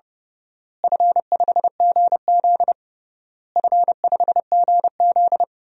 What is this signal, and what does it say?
Morse code, heard as a keyed tone, about 30 wpm, 700 Hz. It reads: F5GZ F5GZ
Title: DXCC-100 Call Signs - 4 Characters - Repeated 3x - 30wpm